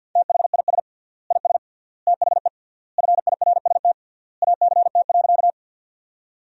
Morse code, heard as a keyed tone, about 51 wpm, 700 Hz. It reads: THIS IS THE FIRST ACT.
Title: 3x Word-Spacing - Sentences from Top 200 Words - 50wpm